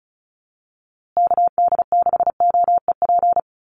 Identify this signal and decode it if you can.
Morse code, heard as a keyed tone, about 34 wpm, 700 Hz. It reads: KD6OEP
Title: USA Call Signs Repeated 2x 35wpm